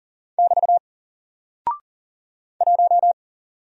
Morse code, heard as a keyed tone, about 40 wpm, 700 Hz. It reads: <BT> 1
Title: Single Character (Letter:Number:Punctuation:Prosign) 40wpm